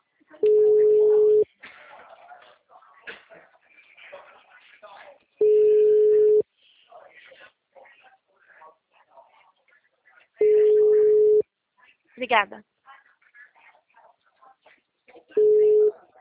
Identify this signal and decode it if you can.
Ringback tone (Europe, 425 Hz), with speech.